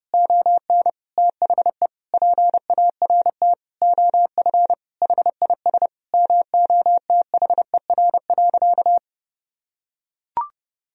Morse code, heard as a keyed tone, about 30 wpm, 700 Hz. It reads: ON THE PART OF HIS MOTHER.
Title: Sentences from Top 1000 Words - Review Part 4 - 30wpm